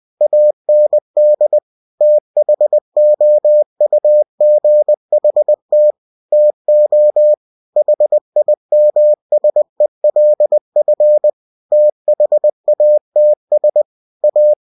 Morse code, heard as a keyed tone, about 20 wpm, 600 Hz. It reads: AND THOUGHT TO HIMSELF THATS A